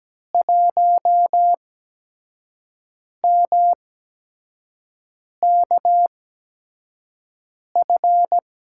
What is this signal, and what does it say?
Morse code, heard as a keyed tone, about 17 wpm, spaced as for 5 wpm, 700 Hz. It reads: 1MKF
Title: USA Call Signs - 5 Characters - 8x Character Spacing - 17wpm